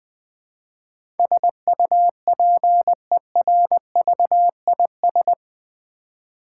Morse code, heard as a keyed tone, about 20 wpm, 700 Hz. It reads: SUPERVIS
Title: Build A Word - 4 Letters At A Time - 11-Letter Words - 20wpm